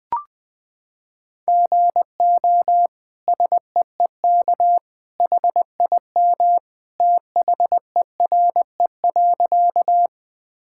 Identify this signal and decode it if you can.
Morse code, heard as a keyed tone, about 20 wpm, 700 Hz. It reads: GO SEEK HIM THERE.
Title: Sentences from Top 1600 Words 20wpm